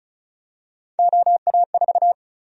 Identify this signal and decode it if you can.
Morse code, heard as a keyed tone, about 34 wpm, 700 Hz. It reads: OA4
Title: USA Call Sign Prefixes - 35wpm